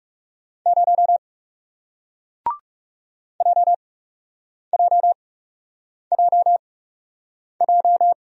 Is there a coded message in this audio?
Yes: Morse code from 0.7 to 8.1 s, heard as a keyed tone, about 40 wpm, spaced as for 8 wpm, 700 Hz. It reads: 0 JJJJ